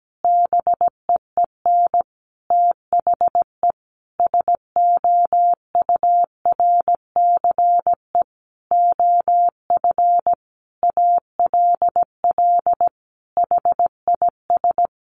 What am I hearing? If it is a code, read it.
Morse code, heard as a keyed tone, about 17 wpm, 700 Hz. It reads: BEEN THE SOURCE OF ALL HIS
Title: Sentences from Top 1200 Words 17wpm